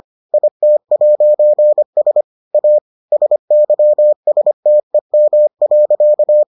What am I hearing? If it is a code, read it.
Morse code, heard as a keyed tone, about 25 wpm, 600 Hz. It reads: IT'S A SYSTEM.